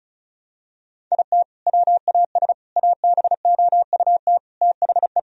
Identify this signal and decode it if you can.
Morse code, heard as a keyed tone, about 36 wpm, 700 Hz. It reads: IT WAS ABOUT THE
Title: Sentences from Top 1000 Words - Review Part 1 - 35wpm